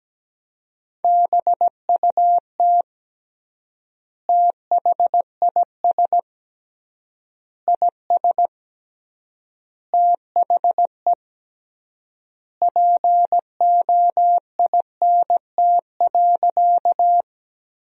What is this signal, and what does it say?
Morse code, heard as a keyed tone, about 17 wpm, 700 Hz. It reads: BUT THIS IS THE POINT.
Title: 3x Word-Spacing - Sentences from Top 200 Words - 17wpm